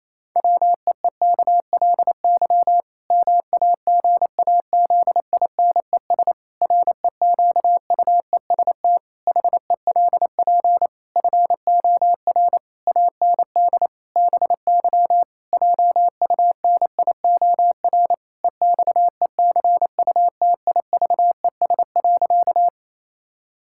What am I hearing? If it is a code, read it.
Morse code, heard as a keyed tone, about 27 wpm, 700 Hz. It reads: WEEKLY MAGAZINES REQUEST HELP FOR AND BY JUNIOR EXECUTIVES.
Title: Pangram Challenge v2 - 28wpm